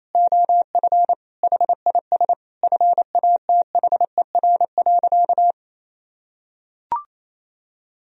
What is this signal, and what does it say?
Morse code, heard as a keyed tone, about 28 wpm, 700 Hz. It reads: OF HIS FATHER.
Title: Sentences from Top 1000 Words - Review Part 2 - 28wpm